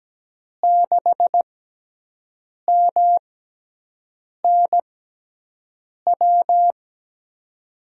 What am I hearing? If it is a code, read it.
Morse code, heard as a keyed tone, about 17 wpm, spaced as for 6 wpm, 700 Hz. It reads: 6MNW
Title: USA Call Signs - 6x Character Spacing - 17wpm